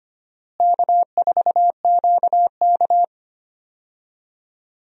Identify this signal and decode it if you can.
Morse code, heard as a keyed tone, about 25 wpm, 700 Hz. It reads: K4QK
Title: USA Call Signs - 4 Characters - Repeated 3x - 25wpm